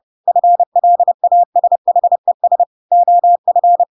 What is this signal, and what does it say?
Morse code, heard as a keyed tone, about 30 wpm, 700 Hz. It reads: FLASHES OF
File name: Pinocchio / Pinocchio ch6 30 30 ik1hge